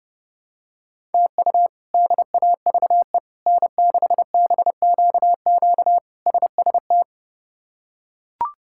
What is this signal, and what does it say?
Morse code, heard as a keyed tone, about 30 wpm, 700 Hz. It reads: TU DAVE N6BQQ SST
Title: K1USN SST QSO Practice - 30wpm